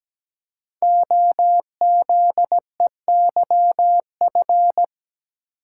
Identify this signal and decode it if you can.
Morse code, heard as a keyed tone, about 17 wpm, 700 Hz. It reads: OZEYF
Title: Code Groups - 5 Letter:Numbers - 17wpm